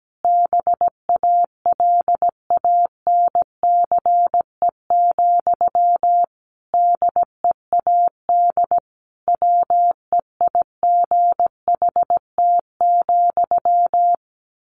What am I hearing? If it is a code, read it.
Morse code, heard as a keyed tone, about 17 wpm, 700 Hz. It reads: BALANCE, DEAD WEIGHT,
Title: A Sound Of Thunder 17wpm